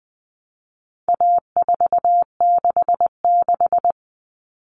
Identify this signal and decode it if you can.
Morse code, heard as a keyed tone, about 20 wpm, 700 Hz. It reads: A466
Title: Common Cut Numbers - 4-Digit 20wpm